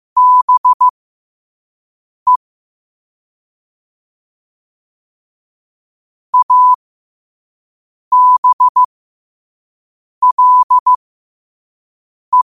Morse code, heard as a keyed tone, about 15 wpm, spaced as for 5 wpm, 1000 Hz. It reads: BE ABLE
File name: qotdfiles / 5wpm